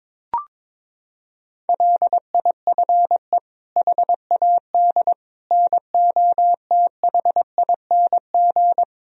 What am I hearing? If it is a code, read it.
Morse code, heard as a keyed tone, about 22 wpm, 700 Hz. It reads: LIFE HAD NOTHING
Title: Sentences from Top 1000 Words - Review Part 4 - 22wpm